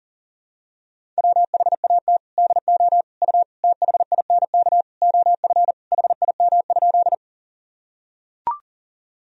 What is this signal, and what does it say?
Morse code, heard as a keyed tone, about 40 wpm, 700 Hz. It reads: WHAT DO U THINK OF HIM?
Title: Sentences from Top 200 Words and Abbreviations 40wpm